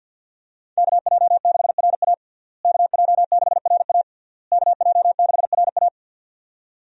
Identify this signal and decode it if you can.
Morse code, heard as a keyed tone, about 50 wpm, 700 Hz. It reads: KJ6RA KJ6RA KJ6RA
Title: USA Call Signs - 5 Characters - Repeated 3x - 50wpm